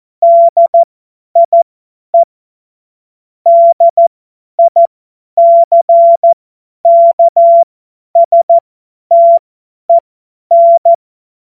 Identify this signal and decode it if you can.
Morse code, heard as a keyed tone, about 14 wpm, spaced as for 10 wpm, 680 Hz. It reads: DIE DICKSTEN
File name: Morsekurs / Witze / Witz 04. Mai 2021, 50 F70